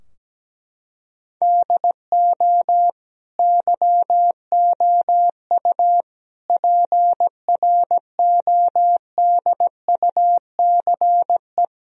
Morse code, heard as a keyed tone, about 17 wpm, 700 Hz. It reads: DO YOU PRODUCE
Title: Sentences from Top 1000 Words - Review Part 4 - 17wpm